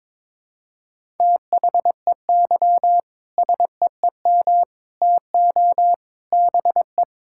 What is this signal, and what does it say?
Morse code, heard as a keyed tone, about 22 wpm, 700 Hz. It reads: THEY SEEM TO BE
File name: practice / 22wpm / Sentences with 3-Word Phrases - 22wpm